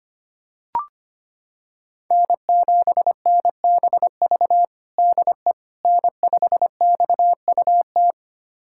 Morse code, heard as a keyed tone, about 25 wpm, 700 Hz. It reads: N7NBV DE N5XUT
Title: USA Callsign DE Callsign - 5 Characters - 25wpm